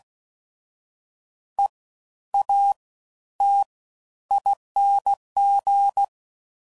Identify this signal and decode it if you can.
Morse code, heard as a keyed tone, about 16 wpm, spaced as for 11 wpm, 800 Hz. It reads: EATING
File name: Suffixes / Special INGs / Fast Ings